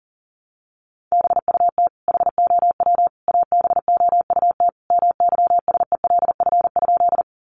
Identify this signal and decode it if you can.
Morse code, heard as a keyed tone, about 40 wpm, 700 Hz. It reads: BUT HOW ABOUT MYSELF?
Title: Sentences from Top 1300 Words 40wpm